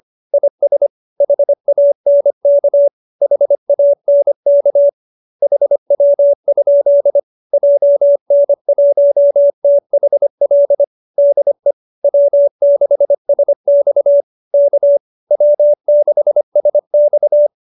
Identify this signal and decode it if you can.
Morse code, heard as a keyed tone, about 25 wpm, 570 Hz. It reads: IS HANK HANK HW? JN1THL DE W6SX K W6SX